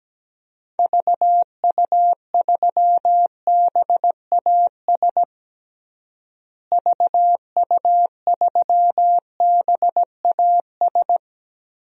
Morse code, heard as a keyed tone, about 17 wpm, 700 Hz. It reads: VU3BAS VU3BAS
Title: DXCC-100 Call Signs - 6 Characters - Repeated 2x - 17wpm